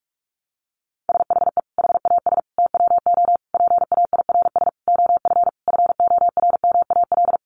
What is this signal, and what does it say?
Morse code, heard as a keyed tone, about 44 wpm, 700 Hz. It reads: SHE HAS TWO PAIRS OF FORMAL
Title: Additional Easy Phrases 45wpm